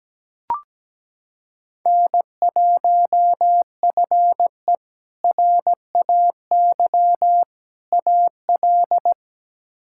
Morse code, heard as a keyed tone, about 17 wpm, 700 Hz. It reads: N1FE RAY AL
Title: K1USN SST QSO Practice - 17wpm